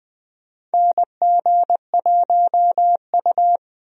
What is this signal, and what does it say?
Morse code, heard as a keyed tone, about 20 wpm, 700 Hz. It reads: NG1U
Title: USA Call Signs - 4 Characters - 20wpm